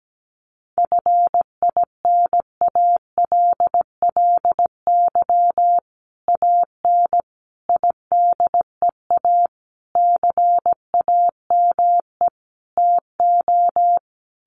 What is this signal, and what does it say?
Morse code, heard as a keyed tone, about 17 wpm, 700 Hz. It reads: FINALLY AN IDEA CAME TO
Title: Sentences from Top 1000 Words 17wpm